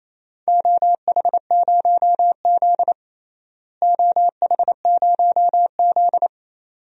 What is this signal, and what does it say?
Morse code, heard as a keyed tone, about 28 wpm, 700 Hz. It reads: OH0Z OH0Z
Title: DXCC-100 Call Signs - 4 Characters - Repeated 2x - 28wpm